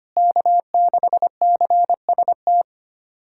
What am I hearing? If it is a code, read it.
Morse code, heard as a keyed tone, about 25 wpm, 700 Hz. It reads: K6CST
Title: USA Callsign DE Callsign - 5 Characters - 25wpm